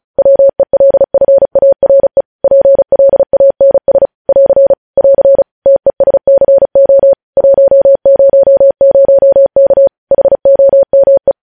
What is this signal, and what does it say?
Morse code, heard as a keyed tone, about 36 wpm, 550 Hz. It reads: WELFARE PLANS <AR> <AR> TESCO 100K HOME
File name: podcast / News headlines-1x-35x35